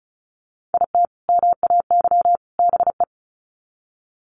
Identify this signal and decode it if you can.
Morse code, heard as a keyed tone, about 34 wpm, 700 Hz. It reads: IT MAY BE
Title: Sets of 3 Words 35wpm